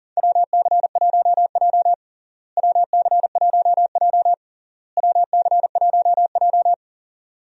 Morse code, heard as a keyed tone, about 40 wpm, 700 Hz. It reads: WC1J WC1J WC1J
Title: USA Call Signs - 4 Characters - Repeated 3x - 40wpm